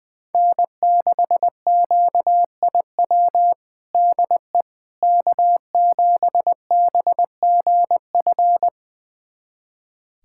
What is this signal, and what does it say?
Morse code, heard as a keyed tone, about 21 wpm, 700 Hz. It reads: N6QIW DE K7BGF
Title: USA Callsign DE Callsign - 5 Characters - 20wpm